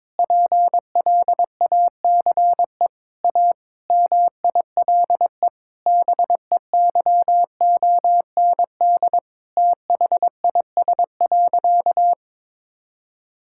Morse code, heard as a keyed tone, about 21 wpm, 700 Hz. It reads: PLACE A MILE BEYOND THIS.
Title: Sentences from Top 1000 Words - Review Part 6 - 22wpm